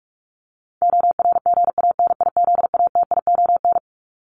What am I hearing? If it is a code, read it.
Morse code, heard as a keyed tone, about 44 wpm, 700 Hz. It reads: ORGANIZATION
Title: Top 501-750 Words 45wpm